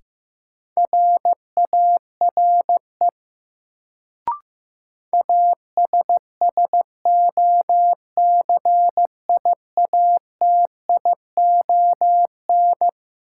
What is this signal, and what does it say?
Morse code, heard as a keyed tone, about 15 wpm, 700 Hz. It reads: RARE ASSOCIATION
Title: Sentences from Top 1100 Words - New Word Drill - 15wpm